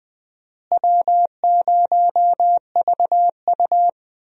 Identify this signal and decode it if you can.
Morse code, heard as a keyed tone, about 20 wpm, 700 Hz. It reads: W0VU
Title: USA Call Signs - 4 Characters - Repeated 2x - 20wpm